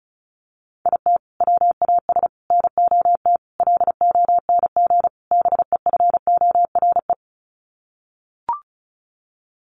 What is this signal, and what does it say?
Morse code, heard as a keyed tone, about 34 wpm, 700 Hz. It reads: IT WAS NOT LONG BEFORE
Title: Sets of 5 Words Encore2 35wpm